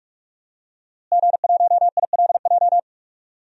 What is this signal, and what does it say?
Morse code, heard as a keyed tone, about 42 wpm, 700 Hz. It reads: G1ILJ
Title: DXCC-100 Call Signs - 5 Characters - Repeated 2x - 45wpm